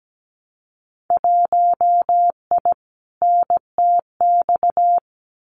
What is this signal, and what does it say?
Morse code, heard as a keyed tone, about 17 wpm, 700 Hz. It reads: 1I NTX
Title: Winter Field Day Exchanges Encore 17wpm